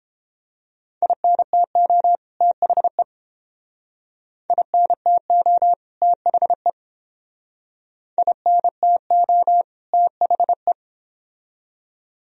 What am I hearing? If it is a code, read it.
Morse code, heard as a keyed tone, about 30 wpm, 700 Hz. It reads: INTO THE INTO THE INTO THE